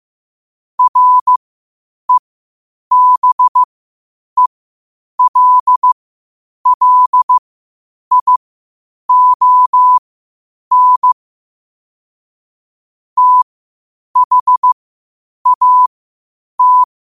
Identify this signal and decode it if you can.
Morse code, heard as a keyed tone, about 15 wpm, spaced as for 8 wpm, 1000 Hz. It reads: REBELLION THAT
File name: qotdfiles / 8wpm